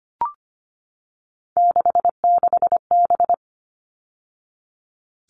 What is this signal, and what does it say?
Morse code, heard as a keyed tone, about 25 wpm, 700 Hz. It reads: 66B
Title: All Cut Numbers - 3-Digit 25wpm